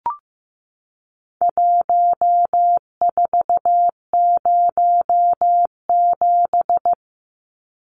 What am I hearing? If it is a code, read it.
Morse code, heard as a keyed tone, about 15 wpm, 700 Hz. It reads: 1407
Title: Numbers - Contest Style - 4-Digit 15wpm